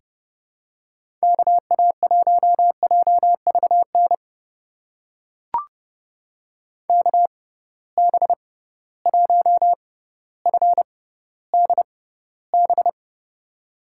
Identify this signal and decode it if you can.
Morse code, heard as a keyed tone, about 30 wpm, spaced as for 15 wpm, 700 Hz. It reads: KA1JVN KB1FDB
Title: USA Call Signs - 6 Characters - 6x Character Spacing - 30wpm